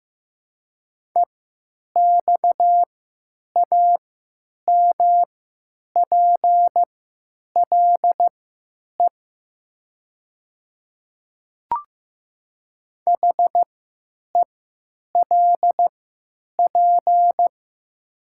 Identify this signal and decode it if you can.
Morse code, heard as a keyed tone, about 15 wpm, spaced as for 9 wpm, 700 Hz. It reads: EXAMPLE HELP
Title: 3x Character Spacing - Top 300 Words - 15wpm